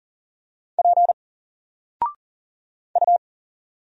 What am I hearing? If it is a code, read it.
Morse code, heard as a keyed tone, about 40 wpm, 700 Hz. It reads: P U
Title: Single Letter-Number 40wpm